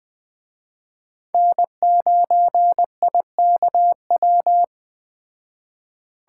Morse code, heard as a keyed tone, about 21 wpm, 700 Hz. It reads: N9IKW
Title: Speed Racing - USA 1x3 Call Signs 17wpm